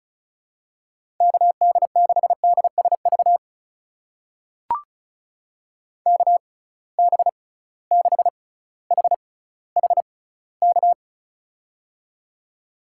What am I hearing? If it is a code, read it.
Morse code, heard as a keyed tone, about 36 wpm, spaced as for 18 wpm, 690 Hz. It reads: KD6DSV KB6HHK
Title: USA Call Signs - 6 Characters - 6x Character Spacing - 35wpm